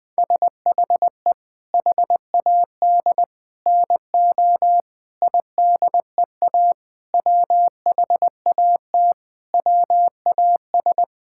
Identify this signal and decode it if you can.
Morse code, heard as a keyed tone, about 20 wpm, 700 Hz. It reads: SHE HAD NO IDEA WHAT WAS
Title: Sentences with 7-Word Phrases - 20wpm